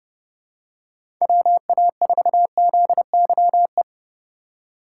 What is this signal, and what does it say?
Morse code, heard as a keyed tone, about 31 wpm, 700 Hz. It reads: WA4ZYE
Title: USA Call Signs - 6 Characters - 4x Character Spacing - 30wpm